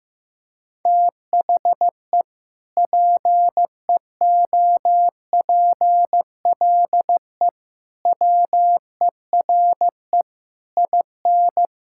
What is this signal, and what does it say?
Morse code, heard as a keyed tone, about 15 wpm, 700 Hz. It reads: THE PEOPLE WERE IN
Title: Sentences from Top 1600 Words 15wpm